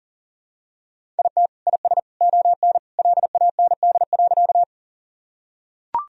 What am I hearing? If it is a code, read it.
Morse code, heard as a keyed tone, about 40 wpm, 700 Hz. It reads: IT IS ON LAND.
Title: Sentences from Top 1000 Words - Review Part 1 - 40wpm